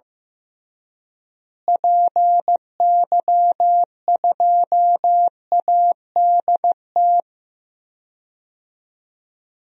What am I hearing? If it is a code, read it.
Morse code, heard as a keyed tone, about 15 wpm, 700 Hz. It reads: PY2ADT
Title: DXCC-100 Call Signs - 6 Characters - Repeated 2x - 15wpm